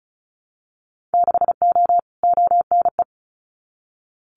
Morse code, heard as a keyed tone, about 36 wpm, 700 Hz. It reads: 6O ONE
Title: Winter Field Day Exchanges Encore 35wpm